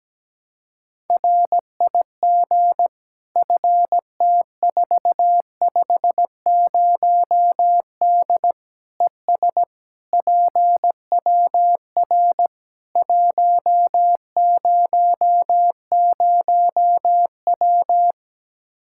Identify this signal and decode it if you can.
Morse code, heard as a keyed tone, about 17 wpm, 700 Hz. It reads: RIG FT450D ES PWR 100W
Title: K1USN SST QSO Practice 17wpm